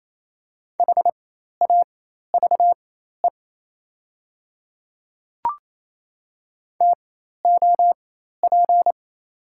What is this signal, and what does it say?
Morse code, heard as a keyed tone, about 28 wpm, spaced as for 13 wpm, 690 Hz. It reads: HAVE TOP
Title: 4x Character Spacing - Top 500 Words - 28wpm